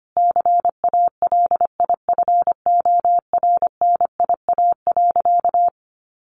Morse code, heard as a keyed tone, about 25 wpm, 700 Hz. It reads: CALIFORNIA.
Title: Easy Phrases 25wpm